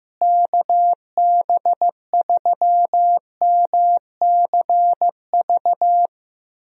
Morse code, heard as a keyed tone, about 15 wpm, 700 Hz. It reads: KB3MCV
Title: USA Call Signs - 6 Characters - Repeated 2x - 15wpm